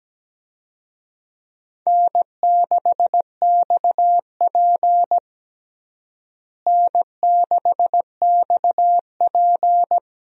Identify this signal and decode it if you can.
Morse code, heard as a keyed tone, about 17 wpm, 700 Hz. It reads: N6XP N6XP
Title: USA Call Signs - 4 Characters - Repeated 2x - 17wpm